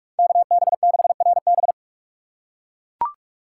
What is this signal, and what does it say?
Morse code, heard as a keyed tone, about 43 wpm, 700 Hz. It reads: KB6RB
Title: USA Call Signs - 5 Characters - 2x Character Spacing - 45wpm